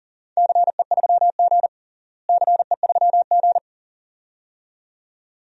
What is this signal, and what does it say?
Morse code, heard as a keyed tone, about 40 wpm, 700 Hz. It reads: CE3G CE3G
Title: DXCC-100 Call Signs - 4 Characters - Repeated 2x - 40wpm